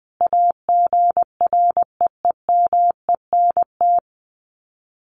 Morse code, heard as a keyed tone, about 20 wpm, 700 Hz. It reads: AGREEMENT
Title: Sets of 5 Words Encore 20wpm